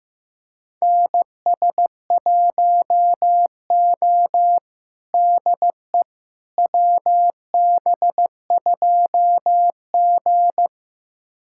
Morse code, heard as a keyed tone, about 16 wpm, 700 Hz. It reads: NS1O DE WB2G
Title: USA Callsign DE Callsign - 4 Characters - 15wpm